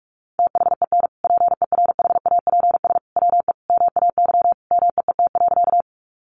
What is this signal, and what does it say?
Morse code, heard as a keyed tone, about 44 wpm, 700 Hz. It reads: THEN PERHAPS WE MAY MEET.
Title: Sentences from Top 1100 Words 45wpm